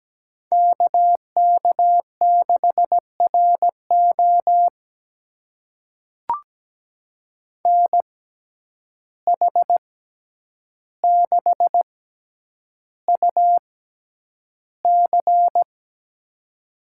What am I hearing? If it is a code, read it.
Morse code, heard as a keyed tone, about 17 wpm, spaced as for 9 wpm, 700 Hz. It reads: KK6RO NH6UC
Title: USA Call Signs - 5 Characters - 6x Character Spacing - 17wpm